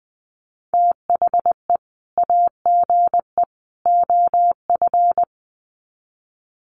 Morse code, heard as a keyed tone, about 20 wpm, 700 Hz. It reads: THE AGE OF
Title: Sets of 3 Words 20wpm